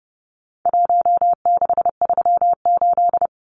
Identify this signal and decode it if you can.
Morse code, heard as a keyed tone, about 28 wpm, 700 Hz. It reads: 1638
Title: Numbers - Contest Style - 4-Digit 30wpm